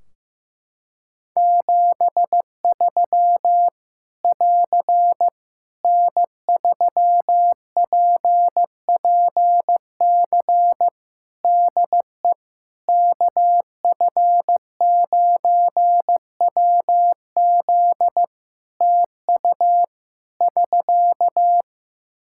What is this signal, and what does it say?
Morse code, heard as a keyed tone, about 15 wpm, 700 Hz. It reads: 73 <AR> N3PPC DE KF9WZ TU <SK>
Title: K1USN SST QSO Practice 15wpm